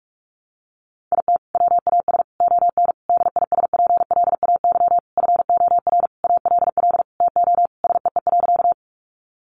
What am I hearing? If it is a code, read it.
Morse code, heard as a keyed tone, about 44 wpm, 700 Hz. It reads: IT WAS ON DISPLAY FOR ALL TO SEE.
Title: Sentences from Top 1200 Words 45wpm